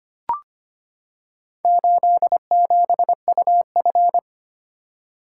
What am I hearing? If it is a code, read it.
Morse code, heard as a keyed tone, about 25 wpm, 700 Hz. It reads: 87UF
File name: practice / 25wpm / Four Letter-Number Combinations - 25wpm